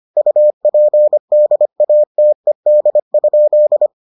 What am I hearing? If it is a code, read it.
Morse code, heard as a keyed tone, about 25 wpm, 600 Hz. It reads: UPDATED?